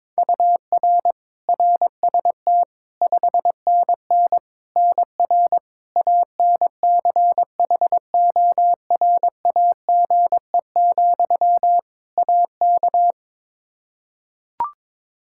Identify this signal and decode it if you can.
Morse code, heard as a keyed tone, about 22 wpm, 700 Hz. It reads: UR RST 5NN NR ANCHORAGE, AK
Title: Practice QSO - Exchange 1 - Beginning - 22wpm